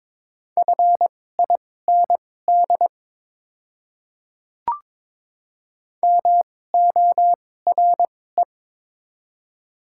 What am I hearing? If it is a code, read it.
Morse code, heard as a keyed tone, about 22 wpm, spaced as for 16 wpm, 700 Hz. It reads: FIND MORE
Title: 2x Character Spacing - Top 100 Words - 22wpm